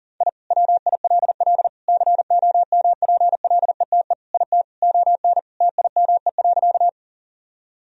Morse code, heard as a keyed tone, about 40 wpm, 700 Hz. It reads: I WILL COMPLETE IT ON TIME.
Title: Sentences from Top 1000 Words - Review Part 2 - 40wpm